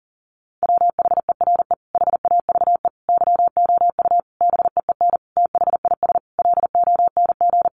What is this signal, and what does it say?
Morse code, heard as a keyed tone, about 40 wpm, 700 Hz. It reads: WHERE HAVE YOU BEEN THIS LONG
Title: Sentences from Top 300 Words 40wpm